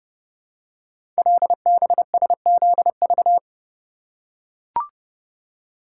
Morse code, heard as a keyed tone, about 29 wpm, 700 Hz. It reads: LBSZV
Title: Code Groups - 5 Letters - 30wpm